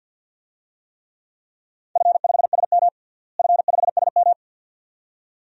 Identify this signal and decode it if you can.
Morse code, heard as a keyed tone, about 52 wpm, 700 Hz. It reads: F5SM F5SM
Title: DXCC-100 Call Signs - 4 Characters - Repeated 2x - 50wpm